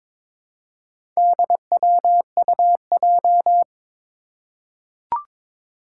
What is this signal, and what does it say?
Morse code, heard as a keyed tone, about 22 wpm, 700 Hz. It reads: DWUJ
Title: Four Challenging Letter-Number Combinations 22wpm